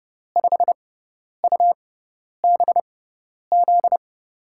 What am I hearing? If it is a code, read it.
Morse code, heard as a keyed tone, about 32 wpm, spaced as for 10 wpm, 700 Hz. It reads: 5UBZ
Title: USA Call Signs - 6 Characters - 6x Character Spacing - 30wpm